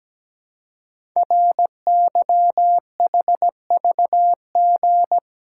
Morse code, heard as a keyed tone, about 17 wpm, 700 Hz. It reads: RYHVG
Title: Code Groups - 5 Letters - 17wpm